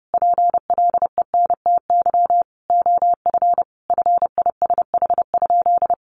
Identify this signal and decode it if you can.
Morse code, heard as a keyed tone, about 30 wpm, 700 Hz. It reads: PLENTY OF FISH?
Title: Sentences from Top 1500 Words 30wpm